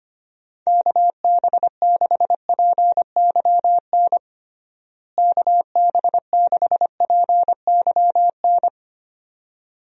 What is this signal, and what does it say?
Morse code, heard as a keyed tone, about 25 wpm, 700 Hz. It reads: KB6PYN KB6PYN
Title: USA Call Signs - 6 Characters - Repeated 2x - 25wpm